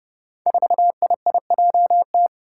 Morse code, heard as a keyed tone, about 30 wpm, 700 Hz. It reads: 4IIJT
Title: Five Challenging Letter-Number Combinations 30wpm